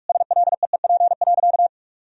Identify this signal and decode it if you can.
Morse code, heard as a keyed tone, about 46 wpm, 700 Hz. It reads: SLEEP.
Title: Sentences from Top 1000 Words - Review Part 2 - 45wpm